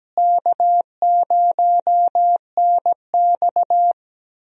Morse code, heard as a keyed tone, about 17 wpm, 700 Hz. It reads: K0NX
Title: USA Call Signs - 4 Characters - Repeated 2x - 17wpm